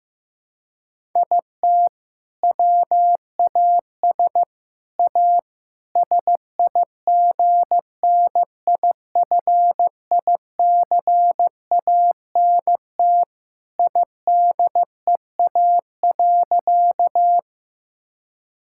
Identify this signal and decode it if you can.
Morse code, heard as a keyed tone, about 15 wpm, 700 Hz. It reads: IT WAS A SIGNIFICANT IDEA.
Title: Sentences from Top 1000 Words - Review Part 6 - 15wpm